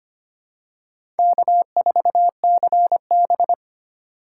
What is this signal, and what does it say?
Morse code, heard as a keyed tone, about 25 wpm, 700 Hz. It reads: K4CB
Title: USA Call Signs - 4 Characters - 2x Character Spacing - 25wpm